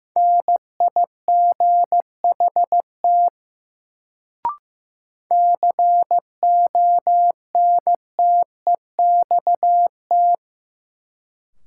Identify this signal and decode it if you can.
Morse code, heard as a keyed tone, about 15 wpm, 700 Hz. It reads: NIGHT CONTEXT
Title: Sets of 4 Words - New Word Drill - 15wpm